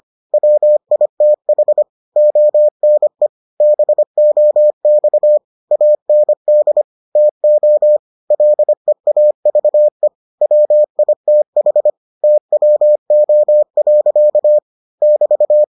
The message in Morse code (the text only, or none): WITH ONE BOX AND TO LEAVE WITH TWO. <BT>